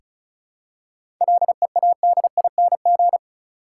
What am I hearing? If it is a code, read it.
Morse code, heard as a keyed tone, about 34 wpm, 700 Hz. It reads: LEADING
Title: Seven Letter Words - 35wpm